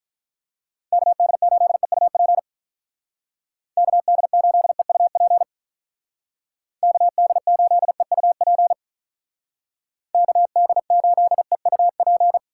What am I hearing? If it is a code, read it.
Morse code, heard as a keyed tone, about 43 wpm, 700 Hz. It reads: KD8EUP KD8EUP KD8EUP KD8EUP
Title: Speed Racing - USA 2x3 Call Signs 35wpm